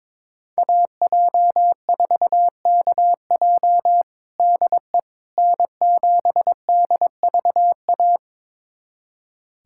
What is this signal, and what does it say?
Morse code, heard as a keyed tone, about 22 wpm, 700 Hz. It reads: AJ4KJ DE N7DVA
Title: USA Callsign DE Callsign - 5 Characters - 22wpm